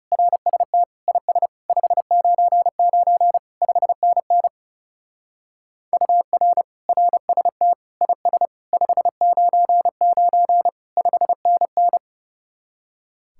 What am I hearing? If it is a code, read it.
Morse code, heard as a keyed tone, about 35 wpm, 700 Hz. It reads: RST IS 599 5NN UR RST IS 599 5NN
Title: Speed-Racing Common QSO Phrases Encore - 30wpm